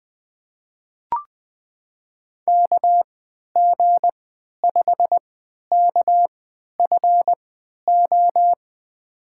Morse code, heard as a keyed tone, about 21 wpm, spaced as for 11 wpm, 700 Hz. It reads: KG5KFO
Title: USA Call Signs - 3x Character Spacing - 20wpm